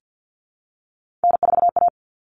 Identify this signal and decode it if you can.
Morse code, heard as a keyed tone, about 50 wpm, 700 Hz. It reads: N4A
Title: Common Cut Numbers - 3-Digit 50wpm